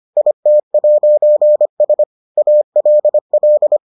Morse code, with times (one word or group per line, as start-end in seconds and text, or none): 0.2-2.0 IT'S
2.4-3.8 ALL